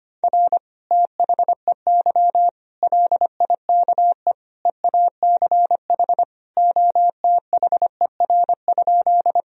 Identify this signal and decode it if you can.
Morse code, heard as a keyed tone, about 25 wpm, 700 Hz. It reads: R THEY LIKE EACH OTHER?
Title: Sentences from Top 200 Words and Abbreviations 25wpm